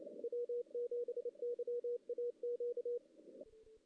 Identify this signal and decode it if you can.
Morse code, heard as a keyed tone, about 29 wpm, 480 Hz, with radio noise. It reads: W7YAQ
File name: 2016 JAN CW NAQP / 2016 Jan naqp k5ot - 0-2